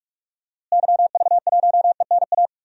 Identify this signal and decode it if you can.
Morse code, heard as a keyed tone, about 44 wpm, 700 Hz. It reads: YV1ENA
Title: DXCC-100 Call Signs - 6 Characters - Repeated 2x - 45wpm